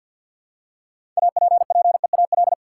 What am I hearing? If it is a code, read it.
Morse code, heard as a keyed tone, about 52 wpm, 700 Hz. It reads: APPEAL
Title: Six Letter Words - 50wpm